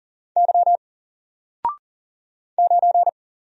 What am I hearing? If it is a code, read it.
Morse code, heard as a keyed tone, about 40 wpm, 700 Hz. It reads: Y 9
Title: Single Character (Letter:Number:Punctuation:Prosign) Alt 40wpm